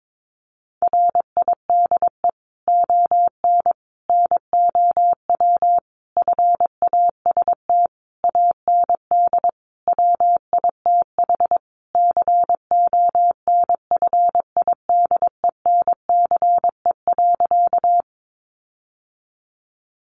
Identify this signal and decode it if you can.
Morse code, heard as a keyed tone, about 22 wpm, 700 Hz. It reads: RIDE ON NOW FAST AND WITH CONFIDENCE.
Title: Sentences from Top 1500 Words 22wpm